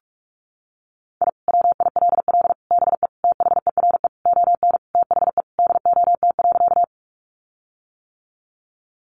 Morse code, heard as a keyed tone, about 44 wpm, 700 Hz. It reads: I WILL BE THERE ON THE DOT.
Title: Sentences from Top 1500 Words 45wpm